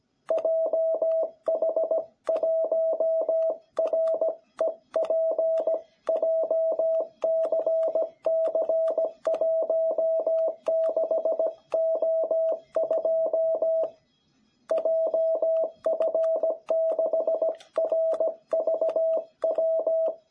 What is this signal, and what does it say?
Morse code, heard as a keyed tone, about 17 wpm, 660 Hz. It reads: JH1REP JCC16O2 JF6RVW
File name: jh8cbh / 2021cw